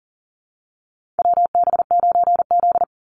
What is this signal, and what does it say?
Morse code, heard as a keyed tone, about 40 wpm, 700 Hz. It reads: WB9Z
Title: Call Signs Repeated 2x - 40wpm